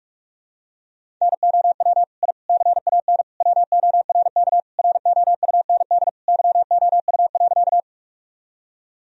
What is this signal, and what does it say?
Morse code, heard as a keyed tone, about 46 wpm, 700 Hz. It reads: NOW I CAN WORK ROUND YOU.
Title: Sentences from Top 1000 Words - Review Part 1 - 45wpm